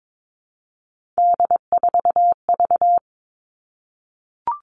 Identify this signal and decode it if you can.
Morse code, heard as a keyed tone, about 22 wpm, 700 Hz. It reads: D4V
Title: All Cut Numbers - 3-Digit 22wpm